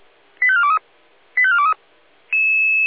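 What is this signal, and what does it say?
Five-tone selective call (ZVEI), heard as quick runs of five tones, tones of about 70 ms. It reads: 85423 85423
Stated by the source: ZVEI five-tone selective call 85849 received on an SDR.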